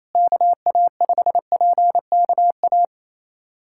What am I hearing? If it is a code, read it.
Morse code, heard as a keyed tone, about 28 wpm, 700 Hz. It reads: KA5PKA
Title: USA Callsign DE Callsign - 6 Characters - 28wpm